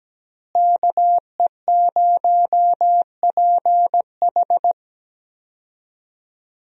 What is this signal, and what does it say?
Morse code, heard as a keyed tone, about 17 wpm, 700 Hz. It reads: KE0PH
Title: USA Call Signs - 5 Characters - Repeated 3x - 17wpm